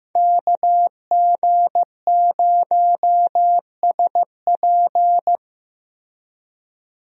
Morse code, heard as a keyed tone, about 15 wpm, 700 Hz. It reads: KG0SP
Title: USA Call Signs - 5 Characters - Repeated 3x - 15wpm